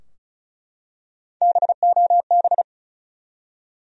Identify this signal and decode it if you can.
Morse code, heard as a keyed tone, about 34 wpm, 700 Hz. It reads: BOB
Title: USA Common Names - Max 5 Letters - 35wpm